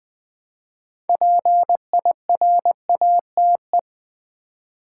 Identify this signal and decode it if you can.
Morse code, heard as a keyed tone, about 20 wpm, 700 Hz. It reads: PIRATE
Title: Build A Word - 3 Letters At A Time - 6-Letter Words - 20wpm